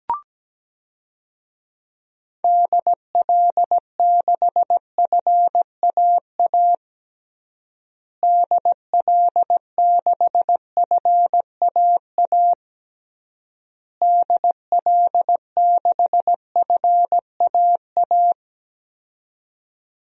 Morse code, heard as a keyed tone, about 17 wpm, 700 Hz. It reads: DL6FAA DL6FAA DL6FAA
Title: DXCC-100 Call Signs - 6 Characters - Repeated 3x - 17wpm